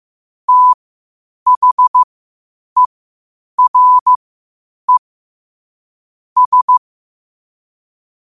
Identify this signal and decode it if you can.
Morse code, heard as a keyed tone, about 15 wpm, spaced as for 8 wpm, 1000 Hz. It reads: THERE S
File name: qotdfiles / 8wpm